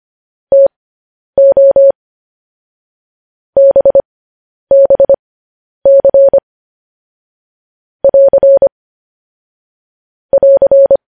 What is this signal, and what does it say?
Morse code, heard as a keyed tone, about 25 wpm, spaced as for 10 wpm, 550 Hz. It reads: TO BBC <AR> <AR>
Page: news-headlines-in-morse-code-at-10-wpm